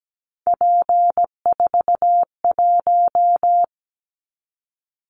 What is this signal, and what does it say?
Morse code, heard as a keyed tone, about 17 wpm, 700 Hz. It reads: P41
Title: Callsign Prefixes for Top 100 DXCC - 17wpm